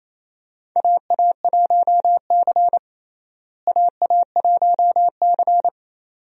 Morse code, heard as a keyed tone, about 28 wpm, 700 Hz. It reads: AA1C AA1C
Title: USA Call Signs - 4 Characters - Repeated 2x - 28wpm